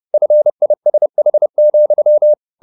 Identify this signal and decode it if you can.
Morse code, heard as a keyed tone, about 30 wpm, 600 Hz. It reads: FISH,
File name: the old man and the sea / the old man and the sea 1 30wpm